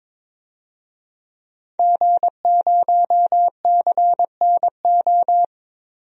Morse code, heard as a keyed tone, about 22 wpm, 700 Hz. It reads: G0CNO
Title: DXCC-100 Call Signs - 5 Characters - Repeated 3x - 22wpm